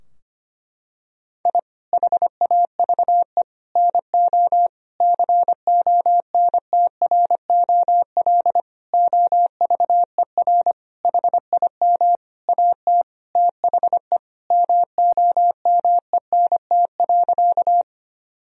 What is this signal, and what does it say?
Morse code, heard as a keyed tone, about 25 wpm, 700 Hz. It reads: I HAVE NO CONTROL OVER HIM AT THE MOMENT.
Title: Sentences from Top 1000 Words - Review Part 3 - 25wpm